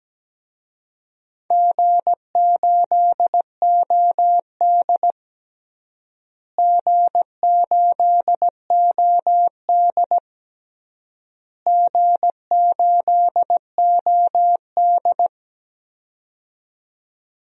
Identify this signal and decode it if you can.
Morse code, heard as a keyed tone, about 17 wpm, 700 Hz. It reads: G8OD G8OD G8OD
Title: DXCC-100 Call Signs - 4 Characters - Repeated 3x - 17wpm